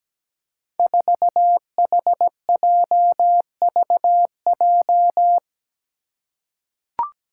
Morse code, heard as a keyed tone, about 17 wpm, 700 Hz. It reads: 4HJVJ
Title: Five Challenging Letter-Number Combinations 17wpm